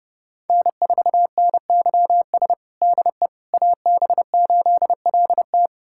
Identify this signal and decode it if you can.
Morse code, heard as a keyed tone, about 32 wpm, 700 Hz. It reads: N4NYS DE AB8LT
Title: USA Callsign DE Callsign - 5 Characters - 30wpm